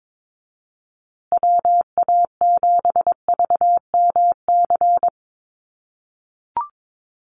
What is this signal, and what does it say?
Morse code, heard as a keyed tone, about 22 wpm, 700 Hz. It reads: WA7VMC
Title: USA Call Signs Repeated 3x 22wpm